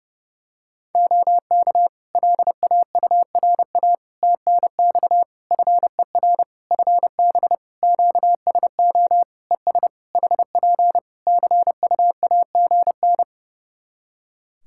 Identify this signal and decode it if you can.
Morse code, heard as a keyed tone, about 30 wpm, 700 Hz. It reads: OK LAURA TNX FER FB QSO ES HP CUAGN
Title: K1USN SST QSO Practice 30wpm